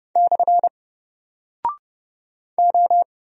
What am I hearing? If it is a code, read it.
Morse code, heard as a keyed tone, about 28 wpm, 700 Hz. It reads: / O
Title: Single Character (Letter:Number:Punctuation:Prosign) 30wpm